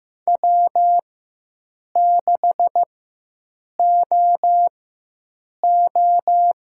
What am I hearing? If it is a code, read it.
Morse code, heard as a keyed tone, about 15 wpm, spaced as for 7 wpm, 700 Hz. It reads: W6OO
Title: USA Call Signs - 4 Characters - 4x Character Spacing - 15wpm